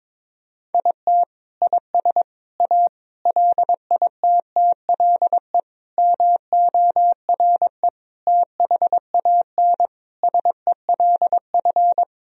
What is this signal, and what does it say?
Morse code, heard as a keyed tone, about 22 wpm, 700 Hz. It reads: IT IS A LITTLE MORE THAN SELF
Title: Sentences from Top 1000 Words - Review Part 1 - 22wpm